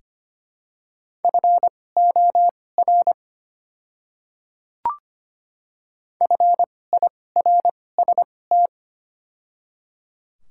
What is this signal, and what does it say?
Morse code, heard as a keyed tone, about 25 wpm, spaced as for 18 wpm, 690 Hz. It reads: FOR FIRST
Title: 2x Character Spacing - Top 100 Words - 25wpm